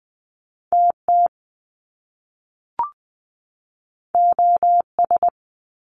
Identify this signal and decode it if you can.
Morse code, heard as a keyed tone, about 20 wpm, 700 Hz. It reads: TT OS
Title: Top Two Letter Combinations in Words 20wpm